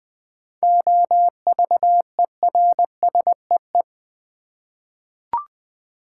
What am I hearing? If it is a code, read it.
Morse code, heard as a keyed tone, about 21 wpm, 690 Hz. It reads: OVERSEE
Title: 7 Letter Words - First 3 Letters Given - 20wpm